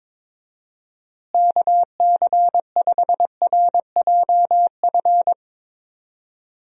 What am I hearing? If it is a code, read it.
Morse code, heard as a keyed tone, about 21 wpm, 700 Hz. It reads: KC5RJF
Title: USA Call Signs - 6 Characters - Repeated 2x - 22wpm